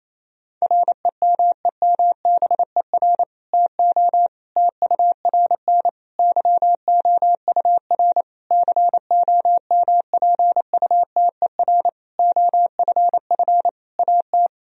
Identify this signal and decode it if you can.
Morse code, heard as a keyed tone, about 28 wpm, 700 Hz. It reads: REMEMBER TO TURN YOUR COMPUTER OFF AT
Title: Sentences from Top 1000 Words - Review Part 3 - 28wpm